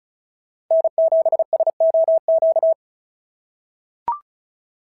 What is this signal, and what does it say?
Morse code, heard as a keyed tone, about 34 wpm, 680 Hz. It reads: N7SOQ
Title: SOTA QSO Practice 35wpm